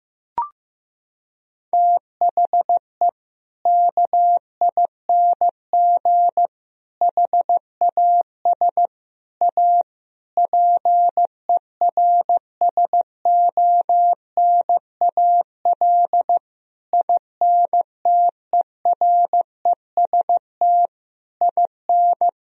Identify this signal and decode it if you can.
Morse code, heard as a keyed tone, about 15 wpm, 700 Hz. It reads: THE KING HAS A PERSONAL INTEREST IN
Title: Sentences from Top 1000 Words - Review Part 4 - 15wpm